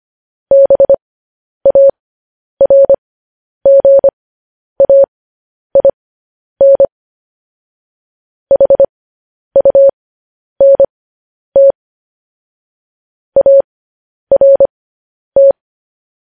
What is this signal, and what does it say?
Morse code, heard as a keyed tone, about 25 wpm, spaced as for 10 wpm, 550 Hz. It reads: BARGAIN HUNT ART
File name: podcast / News headlines-1x-25x10